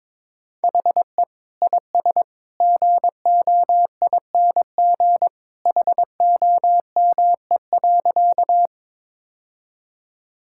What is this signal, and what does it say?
Morse code, heard as a keyed tone, about 22 wpm, 700 Hz. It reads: HE IS GOING HOME.
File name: practice / 22wpm / Sentences with 2-Word Phrases - 22wpm